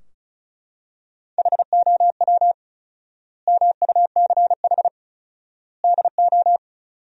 Morse code, heard as a keyed tone, about 36 wpm, 700 Hz. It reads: HOW MUCH DO
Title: 4x Word-Spacing - Sentences from Top 200 Words - 35wpm